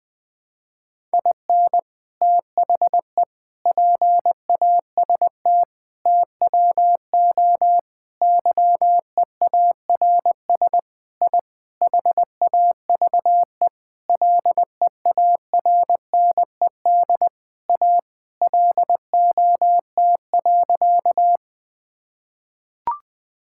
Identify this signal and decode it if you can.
Morse code, heard as a keyed tone, about 20 wpm, 700 Hz. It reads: IN THE PAST TWO YEARS I HAVE LEARNED A LOT.
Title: Sentences with 5-Word Phrases - Spoken Beginning - 20wpm